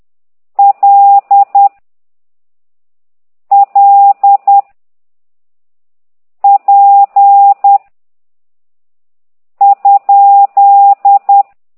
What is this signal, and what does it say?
Morse code, heard as a keyed tone, about 10 wpm, 800 Hz. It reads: L L P ?
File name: ct2idw / exerc9